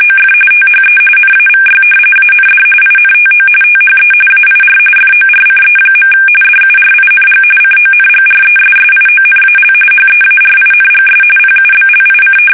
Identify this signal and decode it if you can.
RTTY, about 45.45 baud. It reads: OF THE NAVY AND INTO THE AIR FORCE. DURING CONGRESSIONAL HEARINGS PUBLIC
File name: fsk850 / usnavy5